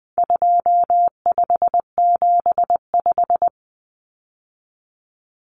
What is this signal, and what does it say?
Morse code, heard as a keyed tone, about 20 wpm, 700 Hz. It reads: 2575
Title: Numbers - 4-Digit 20wpm